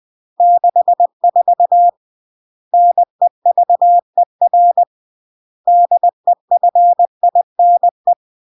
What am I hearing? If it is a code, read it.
Morse code, heard as a keyed tone, about 20 wpm, 700 Hz. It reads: 64 NEVER DEFINE
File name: CW / jodorowsky / 82 commandments 20-20